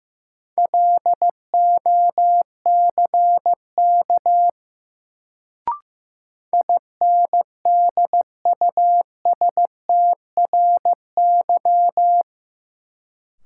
Morse code, heard as a keyed tone, about 15 wpm, 700 Hz. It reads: LOCK INDUSTRY
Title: Sentences from Top 1300 Words - New Word Drill Alt - 15wpm